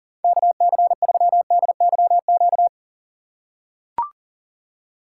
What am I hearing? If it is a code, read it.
Morse code, heard as a keyed tone, about 40 wpm, 690 Hz. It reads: KC3DYQ
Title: USA Call Signs - 6x Character Spacing - 40wpm